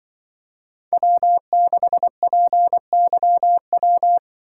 Morse code, heard as a keyed tone, about 23 wpm, 700 Hz. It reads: W6PYW
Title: Speed Racing - USA 1x3 Call Signs 20wpm